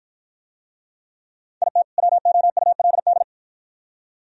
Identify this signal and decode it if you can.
Morse code, heard as a keyed tone, about 49 wpm, 700 Hz. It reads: IT WOULD